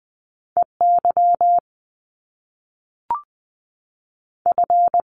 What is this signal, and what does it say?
Morse code, heard as a keyed tone, about 20 wpm, 700 Hz. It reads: E Y F
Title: Top Two Letter Combinations in Words 20wpm